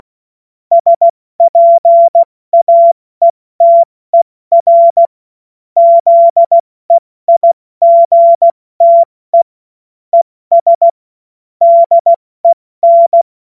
Morse code, heard as a keyed tone, about 16 wpm, spaced as for 14 wpm, 680 Hz. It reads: SPAETER ZEIGTE ES DEN
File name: Morsekurs / Witze / Witz 08. November 2022 70 F80